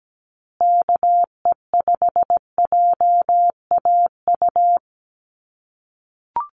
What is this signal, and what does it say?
Morse code, heard as a keyed tone, about 17 wpm, 700 Hz. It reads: KE5JAU
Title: USA Call Signs Repeated 2x 17wpm